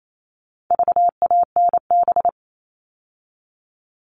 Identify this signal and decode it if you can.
Morse code, heard as a keyed tone, about 28 wpm, 700 Hz. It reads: VANB
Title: All Cut Numbers - 4-Digit 28wpm